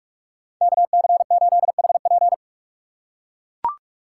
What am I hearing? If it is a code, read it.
Morse code, heard as a keyed tone, about 43 wpm, 700 Hz. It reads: KC8HP
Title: USA Call Signs - 5 Characters - 2x Character Spacing - 45wpm